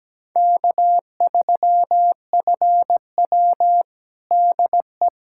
Morse code, heard as a keyed tone, about 17 wpm, 700 Hz. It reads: K3FW DE
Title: USA Callsign DE Callsign - 4 Characters - 17wpm